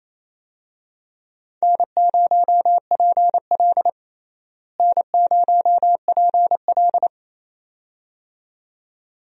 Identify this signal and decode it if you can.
Morse code, heard as a keyed tone, about 28 wpm, 700 Hz. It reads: N0PL N0PL
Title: USA Call Signs - 4 Characters - Repeated 2x - 28wpm